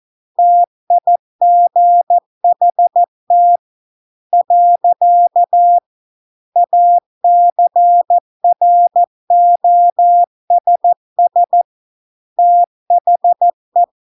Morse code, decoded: TIGHT . ACROSS THE

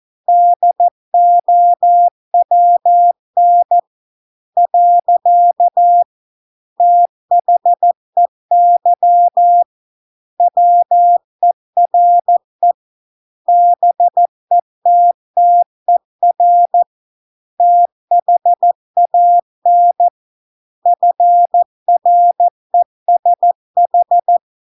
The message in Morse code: DOWN . THEY WERE BETTER THAN FRESH

